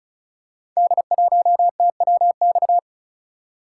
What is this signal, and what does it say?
Morse code, heard as a keyed tone, about 36 wpm, 700 Hz. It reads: D1TWX
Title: Code Groups - 5 Letter:Numbers - 35wpm